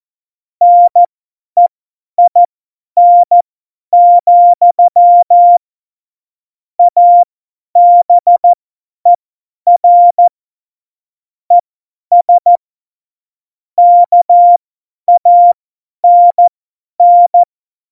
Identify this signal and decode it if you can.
Morse code, heard as a keyed tone, about 14 wpm, spaced as for 10 wpm, 700 Hz. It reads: NEIN, ABER ES KANN
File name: Morsekurs / Witze / Witz 26 Mai 2020, 50 F70